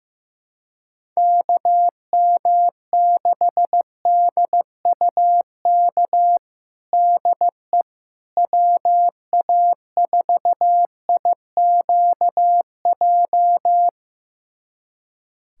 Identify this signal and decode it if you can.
Morse code, heard as a keyed tone, about 15 wpm, 700 Hz. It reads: KM6DUK DE WA4IQJ
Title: USA Callsign DE Callsign - 6 Characters - 15wpm